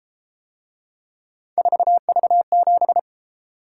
Morse code, heard as a keyed tone, about 33 wpm, 700 Hz. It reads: 4V7